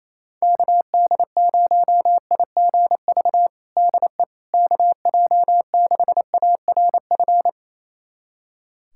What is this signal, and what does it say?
Morse code, heard as a keyed tone, about 27 wpm, 700 Hz. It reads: KD0IGV DE KJ6ARF
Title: USA Callsign DE Callsign - 6 Characters - 28wpm